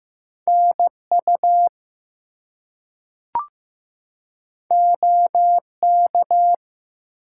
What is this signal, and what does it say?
Morse code, heard as a keyed tone, about 15 wpm, 700 Hz. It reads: NU OK
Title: US States and Canadian Provinces 15wpm